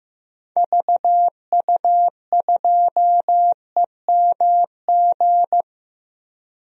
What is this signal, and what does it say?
Morse code, heard as a keyed tone, about 15 wpm, 700 Hz. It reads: VU2EMG
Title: DXCC-100 Call Signs - 6 Characters - Repeated 2x - 15wpm